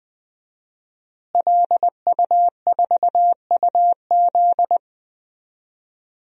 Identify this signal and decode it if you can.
Morse code, heard as a keyed tone, about 20 wpm, 700 Hz. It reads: LU4UZ
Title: DXCC-100 Call Signs - 5 Characters - Repeated 2x - 20wpm